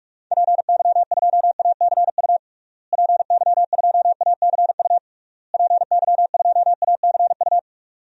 Morse code, heard as a keyed tone, about 46 wpm, 700 Hz. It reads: PY2ACU PY2ACU PY2ACU
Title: DXCC-100 Call Signs - 6 Characters - Repeated 3x - 45wpm